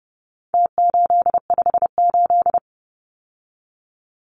Morse code, heard as a keyed tone, about 30 wpm, 700 Hz. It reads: T858
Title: Numbers - Contest Style - 4-Digit 30wpm